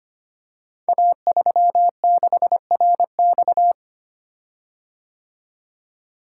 Morse code, heard as a keyed tone, about 25 wpm, 700 Hz. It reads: A36RX
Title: Code Groups - 5 Letter:Numbers - 25wpm